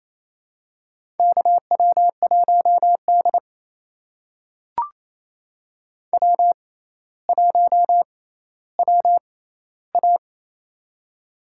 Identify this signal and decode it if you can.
Morse code, heard as a keyed tone, about 28 wpm, spaced as for 14 wpm, 700 Hz. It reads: KW1D W1WA
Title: USA Call Signs - 4 Characters - 6x Character Spacing - 28wpm